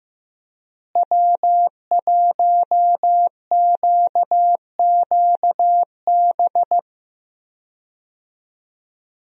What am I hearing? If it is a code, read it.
Morse code, heard as a keyed tone, about 15 wpm, 700 Hz. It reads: W1QQB
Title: USA Call Signs - 5 Characters - Repeated 2x - 15wpm